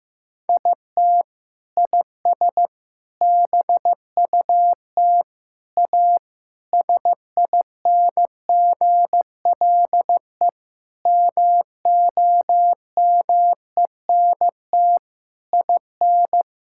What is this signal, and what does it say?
Morse code, heard as a keyed tone, about 15 wpm, 700 Hz. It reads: IT IS BUT A SINGLE MOMENT IN
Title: Sentences from Top 1000 Words - Review Part 6 - 15wpm